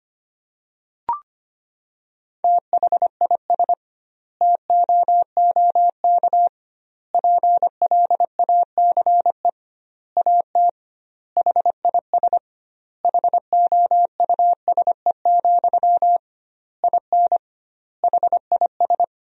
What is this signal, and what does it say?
Morse code, heard as a keyed tone, about 25 wpm, 700 Hz. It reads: THIS TOOK PLACE AT HIS HOUSE, IN HIS
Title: 2x Word-Spacing - Sentences from Top 300 Words - 25wpm